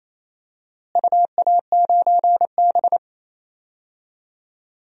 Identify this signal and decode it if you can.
Morse code, heard as a keyed tone, about 27 wpm, 700 Hz. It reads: UA9B
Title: DXCC-100 Call Signs - 4 Characters - Repeated 3x - 28wpm